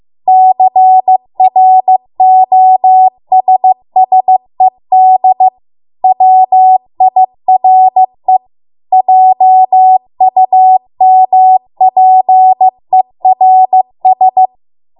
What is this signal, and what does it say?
Morse code, heard as a keyed tone, about 15 wpm, 750 Hz. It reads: CROSSED WIRE JUMPERS